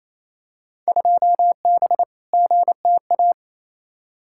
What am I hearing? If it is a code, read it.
Morse code, heard as a keyed tone, about 28 wpm, 700 Hz. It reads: 2B GTA
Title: Field Day Exchanges Encore - 28wpm